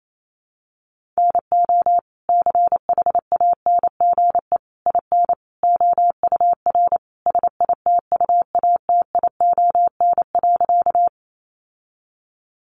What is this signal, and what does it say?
Morse code, heard as a keyed tone, about 28 wpm, 700 Hz. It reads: NO CHANGE IN OUR SITUATION.
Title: Sentences from Top 1200 Words 28wpm